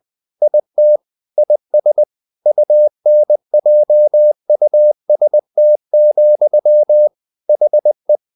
Morse code, heard as a keyed tone, about 20 wpm, 600 Hz. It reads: IT IS UNJUST, HE
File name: the old man and the sea / the old man and the sea part 1 20wpm